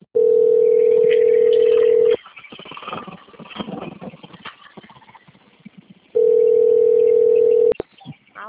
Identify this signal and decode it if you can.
Ringback tone (North America), 440 + 480 Hz.